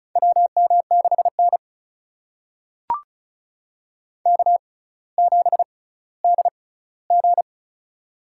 Morse code, heard as a keyed tone, about 35 wpm, spaced as for 18 wpm, 700 Hz. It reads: WM6N K7DG
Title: USA Call Signs - 4 Characters - 6x Character Spacing - 35wpm